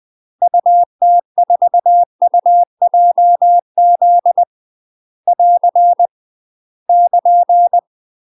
Morse code, heard as a keyed tone, about 20 wpm, 700 Hz. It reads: UT4UJZ <AR> <KN>
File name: UK MASTER QSOs TRAINING / 20WPM / UK MASTER QSOs 0038